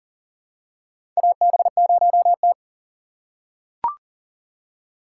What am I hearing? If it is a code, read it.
Morse code, heard as a keyed tone, about 40 wpm, 700 Hz. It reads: AB0T
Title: USA Call Signs - 4 Characters - 3x Character Spacing - 40wpm